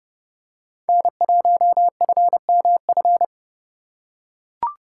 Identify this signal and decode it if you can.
Morse code, heard as a keyed tone, about 29 wpm, 700 Hz. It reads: N1FMF